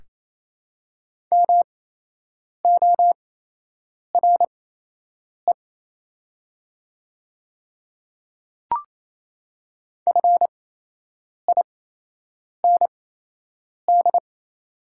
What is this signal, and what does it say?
Morse code, heard as a keyed tone, about 28 wpm, spaced as for 8 wpm, 700 Hz. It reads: MORE FIND
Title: 8x Character Spacing - Top 100 Words - 28wpm